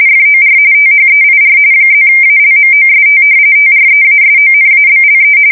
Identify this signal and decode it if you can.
RTTY, about 45.45 baud. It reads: FOR THEIR +ALMOST TREASONABLE
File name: fsk170 / usnavy4